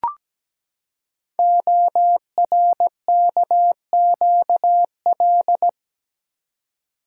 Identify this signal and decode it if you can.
Morse code, heard as a keyed tone, about 17 wpm, 700 Hz. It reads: ORKQL
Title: Code Groups - 5 Letters - 17wpm